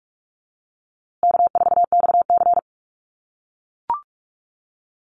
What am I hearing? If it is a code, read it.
Morse code, heard as a keyed tone, about 44 wpm, 700 Hz. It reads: K4XC
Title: USA Call Signs Repeated 3x 45wpm